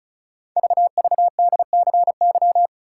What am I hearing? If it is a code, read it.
Morse code, heard as a keyed tone, about 36 wpm, 700 Hz. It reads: VVDCY
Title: Code Groups - 5 Letters - 35wpm